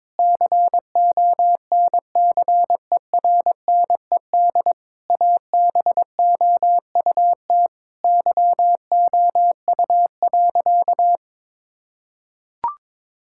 Morse code, heard as a keyed tone, about 22 wpm, 700 Hz. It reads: CONCERNED ABOUT YOU.
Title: Sentences from Top 1000 Words - Review Part 6 - 22wpm